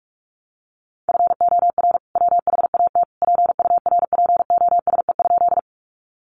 Morse code, heard as a keyed tone, about 44 wpm, 700 Hz. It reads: FOR WHAT PURPOSE?
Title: Sentences from Top 800 Words 45wpm